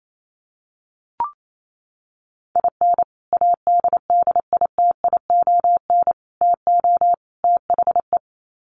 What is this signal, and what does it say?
Morse code, heard as a keyed tone, about 28 wpm, 700 Hz. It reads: IN ADDITION TO THE
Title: Sets of 4 Words 28wpm